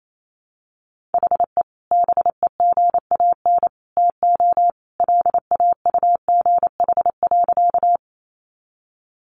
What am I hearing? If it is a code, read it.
Morse code, heard as a keyed tone, about 28 wpm, 700 Hz. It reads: HE BEGAN TO LAUGH.
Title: Sentences from Top 500 Words 28wpm